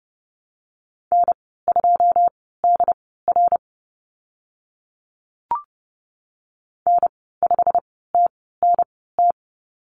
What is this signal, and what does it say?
Morse code, heard as a keyed tone, about 30 wpm, spaced as for 17 wpm, 690 Hz. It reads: N2DR N5TNT
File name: Lesson 15g - Number 5 - Call Signs 30wpm